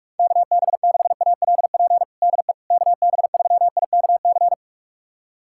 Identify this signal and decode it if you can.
Morse code, heard as a keyed tone, about 46 wpm, 700 Hz. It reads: KB6ALP DE KB3IKC
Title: USA Callsign DE Callsign 45wpm